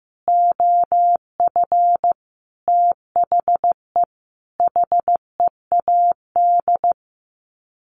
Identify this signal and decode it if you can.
Morse code, heard as a keyed tone, about 15 wpm, 700 Hz. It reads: OF THE HEAD
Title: Sets of 5 Words Encore2 15wpm